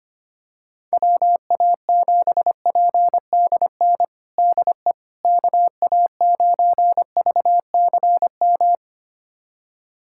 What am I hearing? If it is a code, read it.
Morse code, heard as a keyed tone, about 25 wpm, 700 Hz. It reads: WA7PDN DE KA9VCM
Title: USA Callsign DE Callsign 25wpm